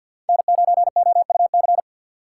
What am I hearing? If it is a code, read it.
Morse code, heard as a keyed tone, about 47 wpm, 700 Hz. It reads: N9OUC
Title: USA Call Signs - 5 Characters - 4x Character Spacing - 50wpm